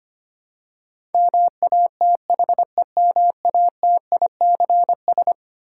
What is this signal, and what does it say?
Morse code, heard as a keyed tone, about 25 wpm, 700 Hz. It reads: MATHEMATICS
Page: morse-code-ninja-practice-25wpm